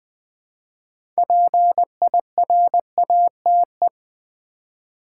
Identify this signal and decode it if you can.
Morse code, heard as a keyed tone, about 20 wpm, 700 Hz. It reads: PIRATE
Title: 6 Letter Words - First 3 Letters Given - 20wpm